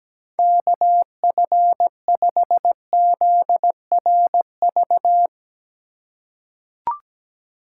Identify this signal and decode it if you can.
Morse code, heard as a keyed tone, about 17 wpm, 700 Hz. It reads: KF5ZRV
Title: USA Call Signs - 6 Characters - 3x Character Spacing - 17wpm